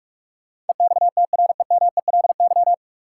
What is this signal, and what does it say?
Morse code, heard as a keyed tone, about 42 wpm, 700 Hz. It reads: EXTREMELY
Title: Sentences from Top 1000 Words - Review Part 7 - New Word Drill - 45wpm